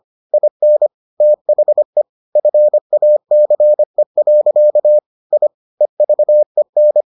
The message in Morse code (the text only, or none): IN THE FACE. I EVEN